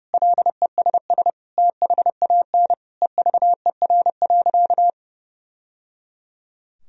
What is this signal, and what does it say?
Morse code, heard as a keyed tone, about 30 wpm, 700 Hz. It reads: LESS THAN EVER.
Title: Sentences from Top 1000 Words - Review Part 2 - 30wpm